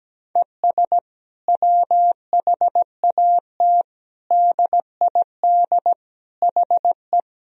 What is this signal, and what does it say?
Morse code, heard as a keyed tone, about 17 wpm, 700 Hz. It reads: ES WHAT DID HE
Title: Sentences from Top 200 Words and Abbreviations 17wpm